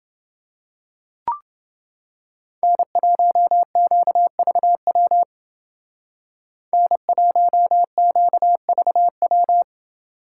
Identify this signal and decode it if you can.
Morse code, heard as a keyed tone, about 28 wpm, 700 Hz. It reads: N1QVW N1QVW